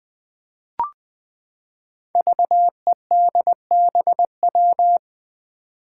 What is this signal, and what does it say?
Morse code, heard as a keyed tone, about 20 wpm, 700 Hz. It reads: VEDBW
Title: Five Challenging Letter-Number Combinations 20wpm